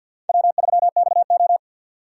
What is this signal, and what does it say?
Morse code, heard as a keyed tone, about 60 wpm, 690 Hz. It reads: W3XO
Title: Speed Racing - USA 1x2 Call Signs 50wpm